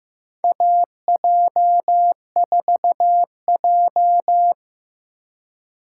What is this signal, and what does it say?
Morse code, heard as a keyed tone, about 15 wpm, 700 Hz. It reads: AJ4J
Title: USA Call Signs - 4 Characters - 3x Character Spacing - 15wpm